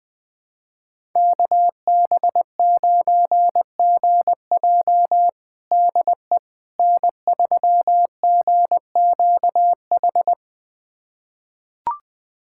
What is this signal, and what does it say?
Morse code, heard as a keyed tone, about 20 wpm, 700 Hz. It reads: KB9GJ DE N3GQH
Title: USA Callsign DE Callsign - 5 Characters - 20wpm